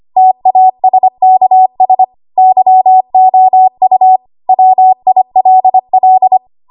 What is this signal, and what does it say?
Morse code, heard as a keyed tone, about 25 wpm, 750 Hz. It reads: TASKS YOU WILL
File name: Morse / 25WPM